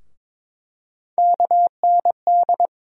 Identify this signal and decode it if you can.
Morse code, heard as a keyed tone, about 22 wpm, 700 Hz. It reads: KND
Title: USA Call Sign Postfixes - 3 Letters 22wpm